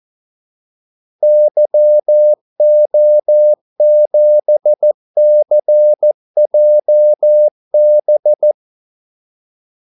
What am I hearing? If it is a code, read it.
Morse code, heard as a keyed tone, about 14 wpm, 600 Hz. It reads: YO7CJB
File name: Callsigns-14wpm